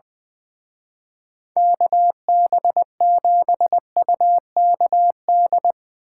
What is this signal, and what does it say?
Morse code, heard as a keyed tone, about 23 wpm, 700 Hz. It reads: KB7UKD
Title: Speed Racing - USA 2x3 Call Signs 20wpm